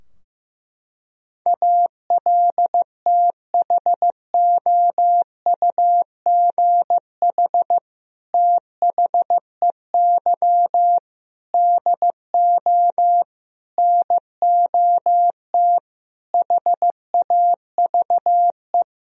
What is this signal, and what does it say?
Morse code, heard as a keyed tone, about 15 wpm, 700 Hz. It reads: ALTHOUGH THEY DO NOT HAVE
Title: Sentences from Top 1000 Words - Review Part 3 - 15wpm